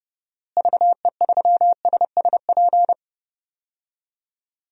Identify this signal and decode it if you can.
Morse code, heard as a keyed tone, about 30 wpm, 700 Hz. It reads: VE3SSP
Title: DXCC-100 Call Signs - 6 Characters - Repeated 3x - 30wpm